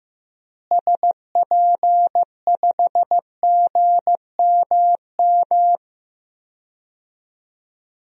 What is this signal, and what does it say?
Morse code, heard as a keyed tone, about 15 wpm, 700 Hz. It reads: SP5GMM
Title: DXCC-100 Call Signs - 6 Characters - Repeated 2x - 15wpm